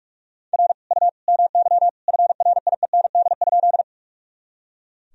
Morse code, heard as a keyed tone, about 41 wpm, 700 Hz. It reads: R U MY FRIEND?
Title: Sentences from Top 300 Words and Abbreviations 45wpm